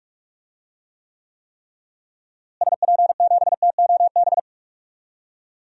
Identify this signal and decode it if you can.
Morse code, heard as a keyed tone, about 45 wpm, 700 Hz. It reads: SP7TOB
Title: DXCC-100 Call Signs - 6 Characters - Repeated 3x - 45wpm